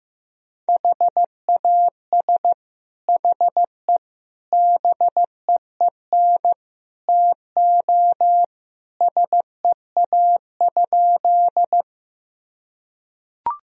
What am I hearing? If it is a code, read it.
Morse code, heard as a keyed tone, about 15 wpm, 700 Hz. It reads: HAS HE BEEN TO SEA?
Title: Sentences from Top 1000 Words - Review Part 1 - 15wpm